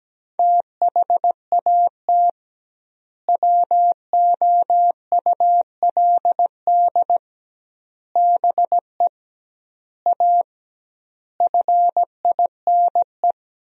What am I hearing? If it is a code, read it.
Morse code, heard as a keyed tone, about 17 wpm, 700 Hz. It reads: THAT WOULD BE A FINE
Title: 2x Word-Spacing - Sentences from Top 500 Words - 17wpm